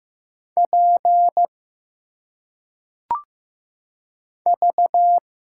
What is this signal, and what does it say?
Morse code, heard as a keyed tone, about 15 wpm, 700 Hz. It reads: P V
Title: Single Letters - Phonetic Pronunciation 15wpm